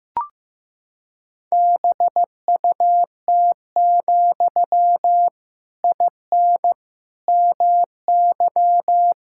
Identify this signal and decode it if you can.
Morse code, heard as a keyed tone, about 15 wpm, 700 Hz. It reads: BUT, IN MY
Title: Sentences from Top 1000 Words - Review Part 5 - 15wpm